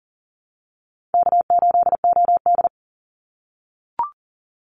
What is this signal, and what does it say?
Morse code, heard as a keyed tone, about 40 wpm, 700 Hz. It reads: K8OD
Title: Easy Call Signs - 40wpm